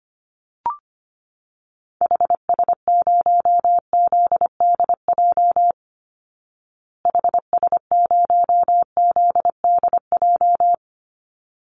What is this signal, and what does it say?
Morse code, heard as a keyed tone, about 25 wpm, 700 Hz. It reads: HS0ZDJ HS0ZDJ
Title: Call Signs Repeated 3x - 25wpm